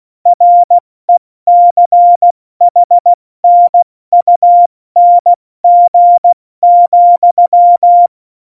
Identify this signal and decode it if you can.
Morse code, heard as a keyed tone, about 16 wpm, spaced as for 14 wpm, 700 Hz. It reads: RECHNUNG,
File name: Morsekurs / Witze / Witz 23. April 2019, 70 F80